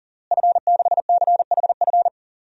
Morse code, heard as a keyed tone, about 40 wpm, 700 Hz. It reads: F6CHF
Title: DXCC-100 Call Signs - 5 Characters - Repeated 3x - 40wpm